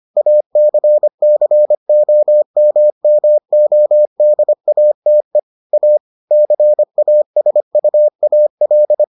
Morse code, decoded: ACCOMMODATE A CASUAL

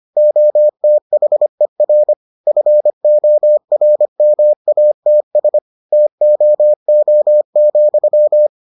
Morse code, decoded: OTHER FORMATS TOO,

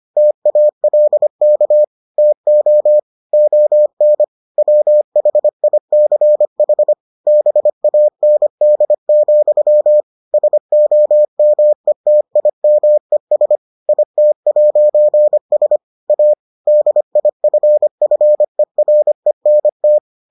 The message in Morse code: TALK TO ON WHICH BAND, SOMETIMES IT'S A DIFFERENT

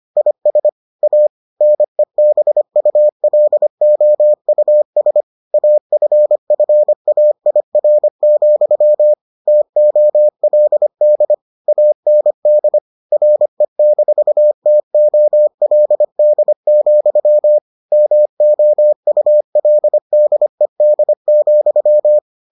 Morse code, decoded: IS A NEBULOUS AFFAIR, TOLD AND RE-TOLD, MOULDED,

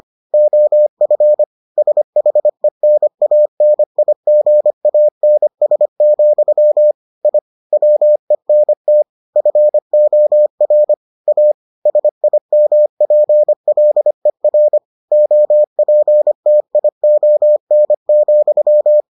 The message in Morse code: OF SHENANIGANS, I WENT FOR A SIMPLER OPTION,

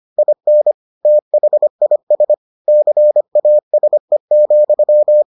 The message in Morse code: IN THIS CASE,

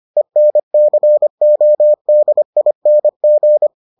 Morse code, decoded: ENCODING

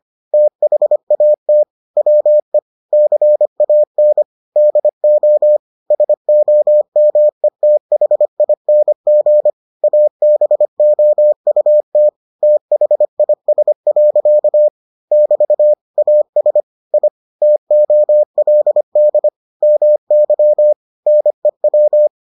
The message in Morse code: THAT WE CAN DO SOMETHING ABOUT THIS. <BT> AS I TOLD MY NEW

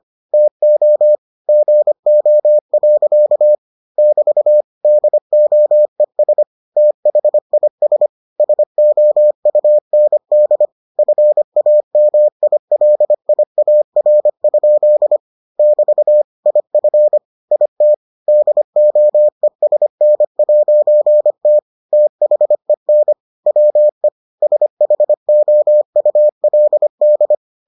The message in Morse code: TO GO. <BT> DOES THIS SOUND FAMILIAR? <BT> IF IT DOESN'T THEN WE SHOULD

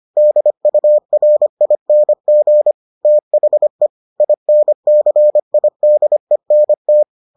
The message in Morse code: DURING THE INCIDENT

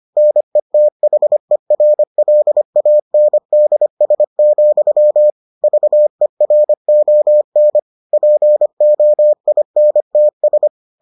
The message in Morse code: NETHERLANDS, VERON POINTS

